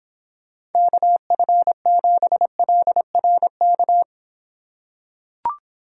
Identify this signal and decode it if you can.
Morse code, heard as a keyed tone, about 26 wpm, 700 Hz. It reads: KF7LRK